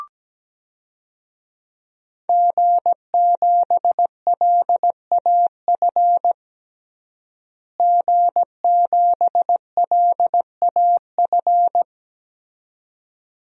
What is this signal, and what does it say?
Morse code, heard as a keyed tone, about 17 wpm, 700 Hz. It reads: G7LAF G7LAF
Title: DXCC-100 Call Signs - 5 Characters - Repeated 2x - 17wpm